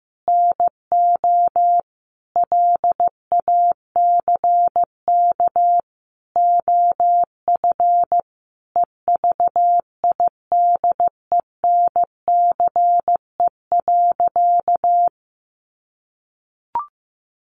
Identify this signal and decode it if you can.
Morse code, heard as a keyed tone, about 15 wpm, 700 Hz. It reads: NO LACK OF EVIDENCE.
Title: Sentences from Top 1300 Words 15wpm